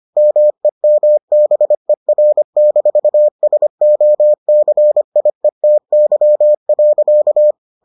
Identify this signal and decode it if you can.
Morse code, heard as a keyed tone, about 25 wpm, 600 Hz. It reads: MEMBER-SOCIETY.